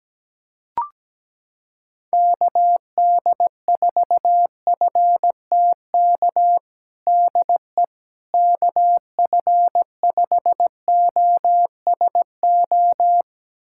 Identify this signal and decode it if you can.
Morse code, heard as a keyed tone, about 17 wpm, 700 Hz. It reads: KD4FTK DE KF5OSO
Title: USA Callsign DE Callsign 17wpm